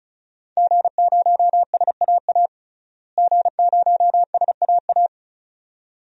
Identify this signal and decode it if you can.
Morse code, heard as a keyed tone, about 36 wpm, 700 Hz. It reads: G0SAA G0SAA
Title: DXCC-100 Call Signs - 5 Characters - Repeated 2x - 35wpm